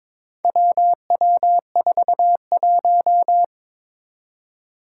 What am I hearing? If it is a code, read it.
Morse code, heard as a keyed tone, about 22 wpm, 700 Hz. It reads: WW41
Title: Four Challenging Letter-Number Combinations 22wpm